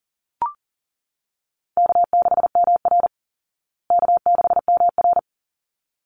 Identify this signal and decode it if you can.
Morse code, heard as a keyed tone, about 40 wpm, 700 Hz. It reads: K6MR K6MR
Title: Easy Call Signs Repeated 2x - 40wpm